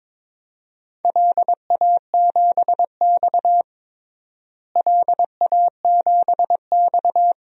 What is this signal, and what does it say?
Morse code, heard as a keyed tone, about 22 wpm, 700 Hz. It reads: LA7X LA7X
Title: DXCC-100 Call Signs - 4 Characters - Repeated 2x - 22wpm